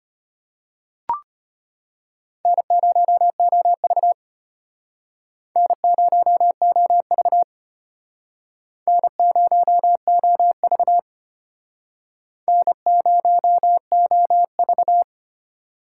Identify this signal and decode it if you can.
Morse code, heard as a keyed tone, about 32 wpm, 700 Hz. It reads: N0OV N0OV N0OV N0OV